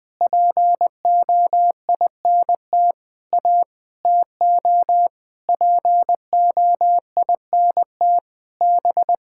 Morse code, heard as a keyed tone, about 20 wpm, 700 Hz. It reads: POINT A TO POINT B
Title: Sentences with 8-Word Phrases - 20wpm